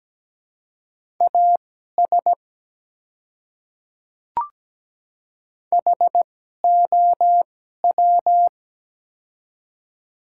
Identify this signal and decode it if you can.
Morse code, heard as a keyed tone, about 17 wpm, spaced as for 12 wpm, 700 Hz. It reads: AS HOW
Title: 2x Character Spacing - Top 100 Words - 17wpm